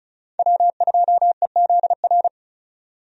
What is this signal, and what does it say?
Morse code, heard as a keyed tone, about 34 wpm, 700 Hz. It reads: W2EZR
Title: USA Call Signs - 5 Characters - 35wpm